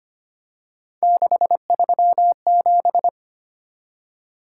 Morse code, heard as a keyed tone, about 25 wpm, 700 Hz. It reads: 637
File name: Three Letter-Number Combinations - 25wpm